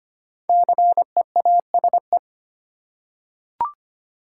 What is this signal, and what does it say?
Morse code, heard as a keyed tone, about 25 wpm, 700 Hz. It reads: CEASE
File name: 5 Letter Words - First 3 Letters Given - 25wpm